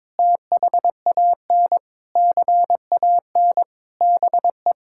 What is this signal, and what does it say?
Morse code, heard as a keyed tone, about 22 wpm, 700 Hz. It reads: THAN CAN BE
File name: practice / 22wpm / Sentences with 8-Word Phrases - 22wpm